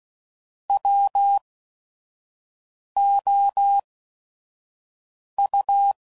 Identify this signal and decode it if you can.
Morse code, heard as a keyed tone, about 16 wpm, spaced as for 5 wpm, 800 Hz. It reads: WOU
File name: CW Code Groups @5WPM / Technical Subjects / AND Gate